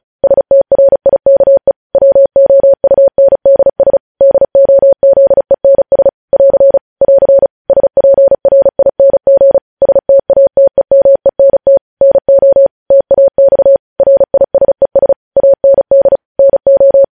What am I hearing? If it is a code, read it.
Morse code, heard as a keyed tone, about 36 wpm, 550 Hz. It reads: STRIKE WOUNDS DOZENS <AR> <AR> SPRING STATEMENT NO TAX RISES AND NO
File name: podcast / News headlines-1x-35x35